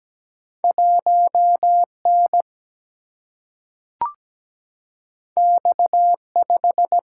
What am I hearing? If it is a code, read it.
Morse code, heard as a keyed tone, about 17 wpm, 700 Hz. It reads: 1N X5
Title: Single-Letter Single-Number 17wpm